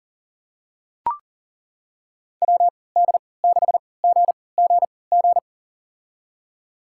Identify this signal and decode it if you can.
Morse code, heard as a keyed tone, about 37 wpm, spaced as for 22 wpm, 700 Hz. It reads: WD6GGG
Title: USA Call Signs - 6 Characters - 3x Character Spacing - 40wpm